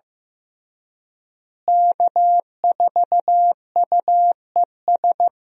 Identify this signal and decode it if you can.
Morse code, heard as a keyed tone, about 15 wpm, 700 Hz. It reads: K4UES
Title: USA Call Signs - 5 Characters - Repeated 3x - 15wpm